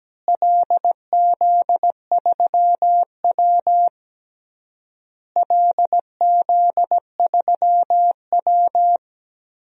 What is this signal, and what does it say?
Morse code, heard as a keyed tone, about 17 wpm, 700 Hz. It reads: LZ3W LZ3W
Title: DXCC-100 Call Signs - 4 Characters - Repeated 3x - 17wpm